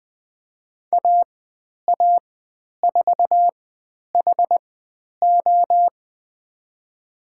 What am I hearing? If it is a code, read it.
Morse code, heard as a keyed tone, about 20 wpm, spaced as for 10 wpm, 700 Hz. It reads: AA4HO
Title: 102 Lesson 17g - Letter u - Call Signs 20wpm